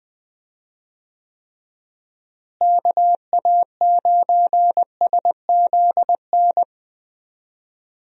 Morse code, heard as a keyed tone, about 20 wpm, 700 Hz. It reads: KA9SZN
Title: USA Call Signs - 6 Characters - Repeated 2x - 20wpm